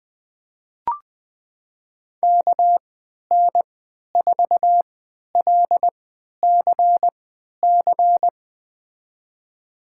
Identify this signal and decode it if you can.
Morse code, heard as a keyed tone, about 20 wpm, spaced as for 11 wpm, 700 Hz. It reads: KN4LCC
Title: USA Call Signs - 6 Characters - 3x Character Spacing - 20wpm